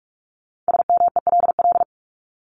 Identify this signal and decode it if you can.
Morse code, heard as a keyed tone, about 44 wpm, 700 Hz. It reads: SMELL
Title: Top 1501-1750 Words 45wpm